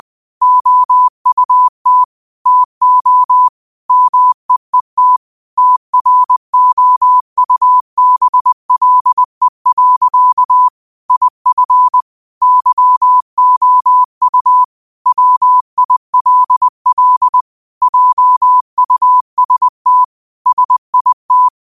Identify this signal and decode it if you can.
Morse code, heard as a keyed tone, about 20 wpm, 1000 Hz. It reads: OUT TO MEET TROUBLE. IF YOU WILL JUST SIT
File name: qotdfiles / 20wpm